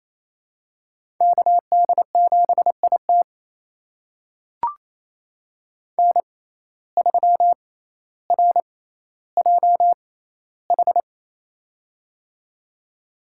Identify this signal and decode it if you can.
Morse code, heard as a keyed tone, about 28 wpm, spaced as for 14 wpm, 700 Hz. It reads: KD7IT N3RJH
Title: USA Call Signs - 5 Characters - 6x Character Spacing - 28wpm